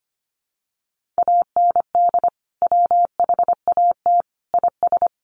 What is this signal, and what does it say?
Morse code, heard as a keyed tone, about 25 wpm, 700 Hz. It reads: AND WHAT IS
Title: Sentences from Top 1500 Words 25wpm